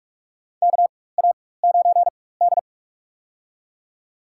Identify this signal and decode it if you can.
Morse code, heard as a keyed tone, about 46 wpm, spaced as for 21 wpm, 700 Hz. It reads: KA9D
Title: USA Call Signs - 4 Characters - 4x Character Spacing - 45wpm